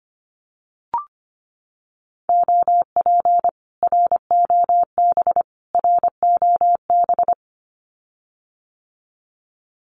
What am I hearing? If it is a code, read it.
Morse code, heard as a keyed tone, about 25 wpm, 700 Hz. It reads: OP ROB ROB
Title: Common QSO Phrases Encore 25wpm